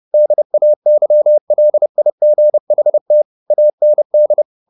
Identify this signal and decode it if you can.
Morse code, heard as a keyed tone, about 30 wpm, 600 Hz. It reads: DAYLIGHT AND
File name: the old man and the sea / the old man and the sea 1 30wpm